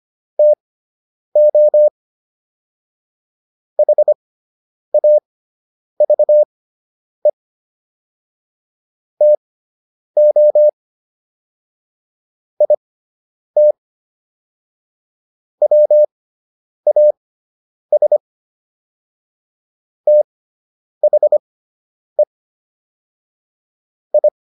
Morse code, heard as a keyed tone, about 25 wpm, spaced as for 9 wpm, 600 Hz. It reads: TO HAVE TO IT WAS THE I